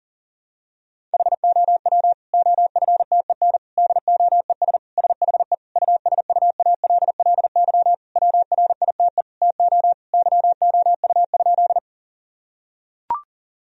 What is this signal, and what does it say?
Morse code, heard as a keyed tone, about 40 wpm, 700 Hz. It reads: HOW OFTEN DOES SHE USUALLY WRITE TO YOU?
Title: Sentences from Top 1000 Words - Review Part 3 - 40wpm